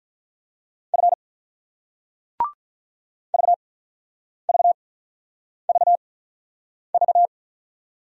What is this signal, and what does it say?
Morse code, heard as a keyed tone, about 48 wpm, spaced as for 9 wpm, 700 Hz. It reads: F VVVV